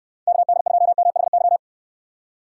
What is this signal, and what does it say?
Morse code, heard as a keyed tone, about 66 wpm, 700 Hz. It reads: KD2MHY